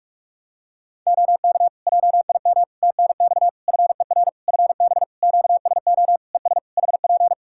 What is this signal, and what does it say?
Morse code, heard as a keyed tone, about 46 wpm, 700 Hz. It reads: OK JIM TNX FER FB QSO ES HP
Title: Practice QSO - Exchange 4 - Ending - 45wpm